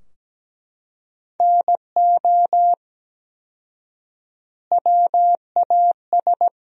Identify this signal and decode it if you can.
Morse code, heard as a keyed tone, about 17 wpm, 700 Hz. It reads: NO WAS
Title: 4x Word-Spacing - Sentences from Top 100 Words - 17wpm